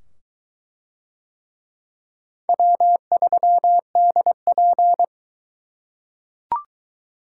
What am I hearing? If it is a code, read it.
Morse code, heard as a keyed tone, about 22 wpm, 700 Hz. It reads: W3DP